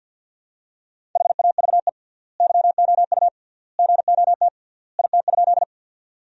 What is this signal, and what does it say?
Morse code, heard as a keyed tone, about 50 wpm, 700 Hz. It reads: HAVE YOU GOT IT?
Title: 3x Word-Spacing - Sentences from Top 300 Words - 50wpm